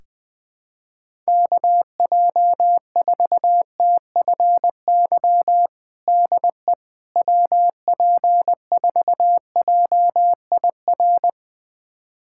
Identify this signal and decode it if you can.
Morse code, heard as a keyed tone, about 20 wpm, 700 Hz. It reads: KJ4TFY DE WP4JIR
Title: USA Callsign DE Callsign 20wpm